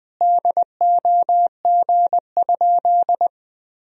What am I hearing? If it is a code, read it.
Morse code, heard as a keyed tone, about 21 wpm, 700 Hz. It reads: DOG?
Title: Sentences from Top 400 Words and Abbreviations 20wpm